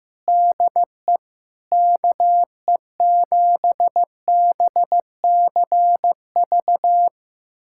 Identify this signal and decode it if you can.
Morse code, heard as a keyed tone, about 15 wpm, 700 Hz. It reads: DE KE7BCV
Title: USA Callsign DE Callsign - 6 Characters - 15wpm